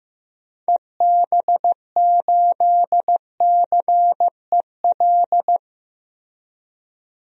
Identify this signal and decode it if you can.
Morse code, heard as a keyed tone, about 15 wpm, 700 Hz. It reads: EB8CEL
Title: DXCC-100 Call Signs - 6 Characters - 15wpm